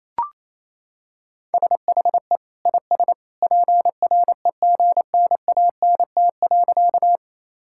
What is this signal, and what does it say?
Morse code, heard as a keyed tone, about 27 wpm, 700 Hz. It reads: SHE IS PREGNANT.
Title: Sentences from Top 1000 Words - Review Part 6 - 28wpm